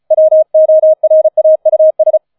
Morse code, heard as a keyed tone, about 36 wpm, 630 Hz. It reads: WORAUS
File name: 004 Vierter Streich 35wpm